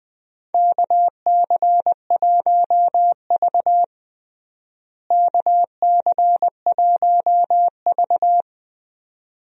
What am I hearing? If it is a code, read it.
Morse code, heard as a keyed tone, about 20 wpm, 700 Hz. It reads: KC1V KC1V
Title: USA Call Signs - 4 Characters - Repeated 2x - 20wpm